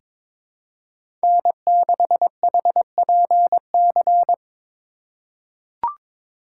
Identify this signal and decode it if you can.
Morse code, heard as a keyed tone, about 22 wpm, 700 Hz. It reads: N6HPC
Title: USA Call Signs - 4x Character Spacing - 22wpm